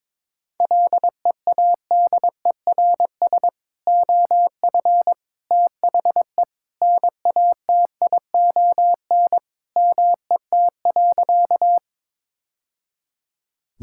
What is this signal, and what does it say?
Morse code, heard as a keyed tone, about 22 wpm, 700 Hz. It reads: LEADERS OF THE NATION MET.
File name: practice / 22wpm / Sentences with 3-Word Phrases - 22wpm